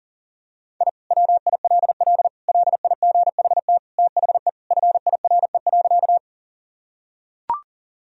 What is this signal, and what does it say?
Morse code, heard as a keyed tone, about 37 wpm, 700 Hz. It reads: I WILL LIGHT THE FIRE.
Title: Sentences from Top 1000 Words - Review Part 1 - 40wpm